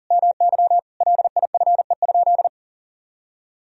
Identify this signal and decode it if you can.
Morse code, heard as a keyed tone, about 37 wpm, 700 Hz. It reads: MY LIFE?
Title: Sentences from Top 300 Words and Abbreviations 40wpm